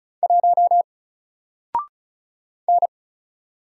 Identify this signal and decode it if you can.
Morse code, heard as a keyed tone, about 33 wpm, 700 Hz. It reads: 1 N
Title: Single Letter-Number 35wpm